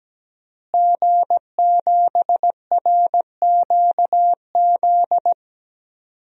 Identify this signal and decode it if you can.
Morse code, heard as a keyed tone, about 17 wpm, 700 Hz. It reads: G7RQZ
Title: DXCC-100 Call Signs - 5 Characters - Repeated 2x - 17wpm